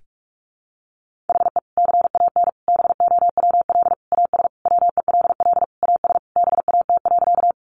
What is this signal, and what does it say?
Morse code, heard as a keyed tone, about 44 wpm, 700 Hz. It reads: HE CAN BOWL AS WELL AS BAT.
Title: Sentences from Top 1500 Words 45wpm